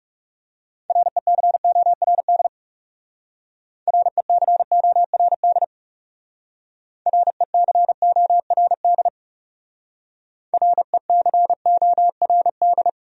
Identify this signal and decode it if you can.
Morse code, heard as a keyed tone, about 39 wpm, 700 Hz. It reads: RECORD RECORD RECORD RECORD